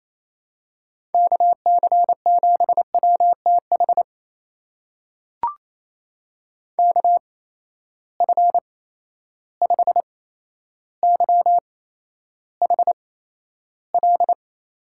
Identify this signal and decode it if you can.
Morse code, heard as a keyed tone, about 27 wpm, spaced as for 12 wpm, 700 Hz. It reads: KC7WTH KF5YHL
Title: USA Call Signs - 6 Characters - 8x Character Spacing - 28wpm